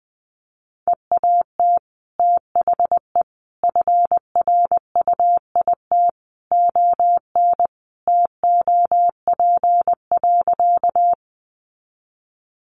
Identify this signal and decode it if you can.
Morse code, heard as a keyed tone, about 20 wpm, 700 Hz. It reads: EAT THE FRUIT ON TOP.
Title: Sentences from Top 1300 Words 20wpm